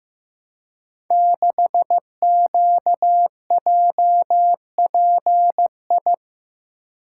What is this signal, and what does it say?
Morse code, heard as a keyed tone, about 15 wpm, 700 Hz. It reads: 6QJPI
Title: Code Groups - 5 Letter:Numbers - 15wpm